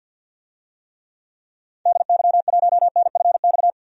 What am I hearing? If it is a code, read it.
Morse code, heard as a keyed tone, about 48 wpm, 700 Hz. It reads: DY1NFX
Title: DXCC-100 Call Signs - 6 Characters - 50wpm